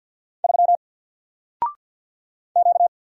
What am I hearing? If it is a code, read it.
Morse code, heard as a keyed tone, about 50 wpm, 700 Hz. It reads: 3 Q
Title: Single Letter-Number 50wpm